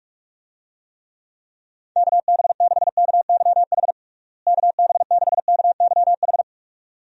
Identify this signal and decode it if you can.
Morse code, heard as a keyed tone, about 46 wpm, 700 Hz. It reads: KB6KYH KB6KYH
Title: USA Call Signs - 6 Characters - Repeated 2x - 45wpm